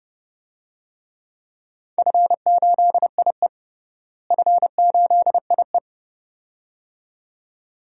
Morse code, heard as a keyed tone, about 30 wpm, 700 Hz. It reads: F8IE F8IE
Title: DXCC-100 Call Signs - 4 Characters - Repeated 2x - 30wpm